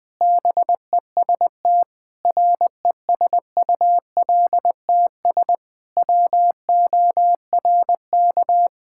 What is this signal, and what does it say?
Morse code, heard as a keyed tone, about 20 wpm, 700 Hz. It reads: BEST RESULTS WORK
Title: Sentences with 3-Word Phrases - Spoken Beginning - 20wpm